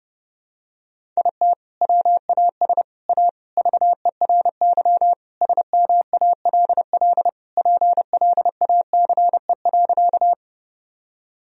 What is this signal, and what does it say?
Morse code, heard as a keyed tone, about 30 wpm, 700 Hz. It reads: IT WAS A VERY SMALL PLACE.
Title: Sentences from Top 1000 Words - Review Part 1 - 30wpm